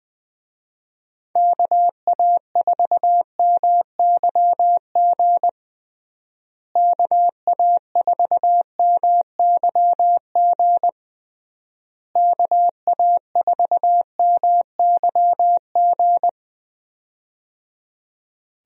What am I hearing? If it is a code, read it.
Morse code, heard as a keyed tone, about 20 wpm, 700 Hz. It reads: KA4MYG KA4MYG KA4MYG
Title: USA Call Signs - 6 Characters - Repeated 3x - 20wpm